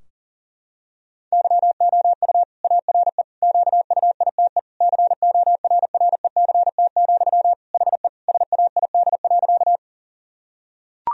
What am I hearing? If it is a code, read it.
Morse code, heard as a keyed tone, about 40 wpm, 700 Hz. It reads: YOU ARE QUITE CORRECT, HE SAID.
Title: Sentences from Top 1000 Words - Review Part 2 - 40wpm